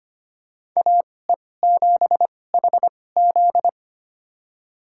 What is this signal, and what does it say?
Morse code, heard as a keyed tone, about 25 wpm, spaced as for 18 wpm, 700 Hz. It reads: AE7HZ
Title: USA Call Signs - 5 Characters - 2x Character Spacing - 25wpm